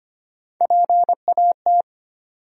Morse code, heard as a keyed tone, about 25 wpm, 700 Hz. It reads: PAT
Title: USA Common Names - Max 3 Letters - 25wpm